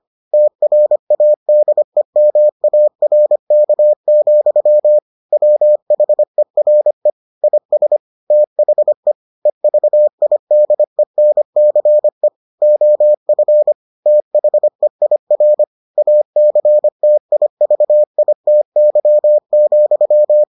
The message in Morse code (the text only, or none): TRADEMARK, WHERE IS THE EVIDENCE OF THEIR ACTIVITY,